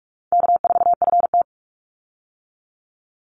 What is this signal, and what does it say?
Morse code, heard as a keyed tone, about 44 wpm, 700 Hz. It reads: K4FT
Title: Easy Call Signs - 45wpm